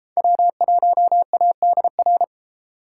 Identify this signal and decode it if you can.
Morse code, heard as a keyed tone, about 29 wpm, 700 Hz. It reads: W1ADR